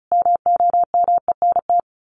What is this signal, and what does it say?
Morse code, heard as a keyed tone, about 34 wpm, 700 Hz. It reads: MOMENT
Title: Sets of 5 Words Encore 35wpm